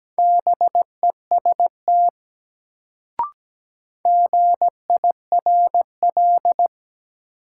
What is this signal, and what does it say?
Morse code, heard as a keyed tone, about 17 wpm, 700 Hz. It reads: BEST GIRL
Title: Sentences from Top 400 Words - New Word Drill - 17wpm